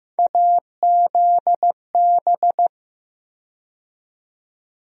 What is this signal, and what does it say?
Morse code, heard as a keyed tone, about 15 wpm, 700 Hz. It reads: AZB
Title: USA Call Sign Postfixes - 3 Letters 15wpm